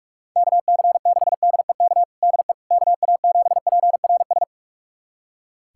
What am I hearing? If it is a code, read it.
Morse code, heard as a keyed tone, about 46 wpm, 700 Hz. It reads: KC6DEK DE KA7PRS
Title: USA Callsign DE Callsign - 6 Characters - 45wpm